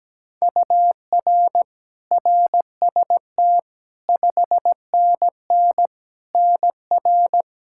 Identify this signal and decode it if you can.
Morse code, heard as a keyed tone, about 17 wpm, 700 Hz. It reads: UR RST 5NN NR
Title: Practice QSO - Exchange 1 - Beginning - 17wpm